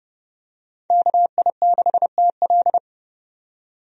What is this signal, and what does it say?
Morse code, heard as a keyed tone, about 30 wpm, 700 Hz. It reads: KI6TL
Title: USA Call Signs - 5 Characters - 30wpm